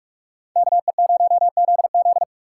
Morse code, heard as a keyed tone, about 45 wpm, 700 Hz. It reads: KE0ZZ